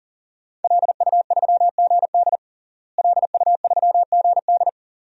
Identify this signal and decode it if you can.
Morse code, heard as a keyed tone, about 39 wpm, 700 Hz. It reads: LU3GD LU3GD
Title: DXCC-100 Call Signs - 5 Characters - Repeated 2x - 40wpm